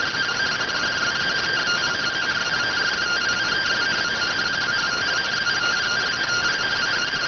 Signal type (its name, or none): two-tone teleprinter signal